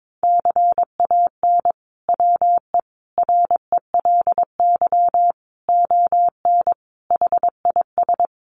Morse code, heard as a keyed tone, about 22 wpm, 700 Hz. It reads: CAN WE RELY ON HIS
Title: Sentences from Top 1200 Words 22wpm